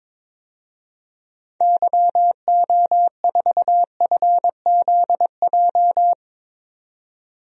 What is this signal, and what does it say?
Morse code, heard as a keyed tone, about 21 wpm, 700 Hz. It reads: YO4FZJ
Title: DXCC-100 Call Signs - 6 Characters - 22wpm